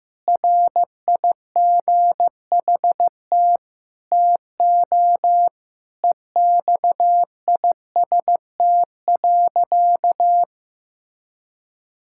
Morse code, heard as a keyed tone, about 16 wpm, 700 Hz. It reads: RIGHT TO EXIST.
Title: Sentences from Top 1000 Words - Review Part 5 - 15wpm